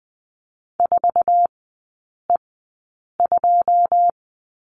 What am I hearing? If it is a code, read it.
Morse code, heard as a keyed tone, about 20 wpm, spaced as for 8 wpm, 700 Hz. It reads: 4E2
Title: 081 Lesson 14e - Number 2 - Three Characters 20wpm